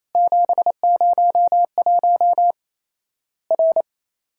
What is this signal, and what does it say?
Morse code, heard as a keyed tone, about 28 wpm, 700 Hz. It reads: 701 R
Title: Worked All Europe DX Contest - QTC Exchange - 28wpm